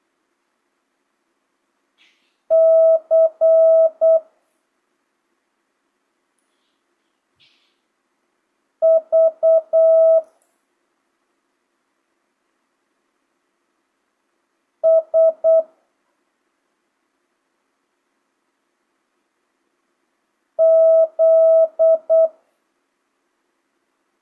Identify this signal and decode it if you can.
Morse code, heard as a keyed tone, about 8 wpm, 650 Hz. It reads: C V S Z
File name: jh8cbh / 2025cw